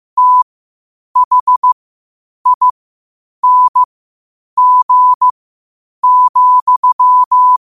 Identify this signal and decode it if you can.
Morse code, heard as a keyed tone, about 15 wpm, spaced as for 8 wpm, 1000 Hz. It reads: THING,
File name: qotdfiles / 8wpm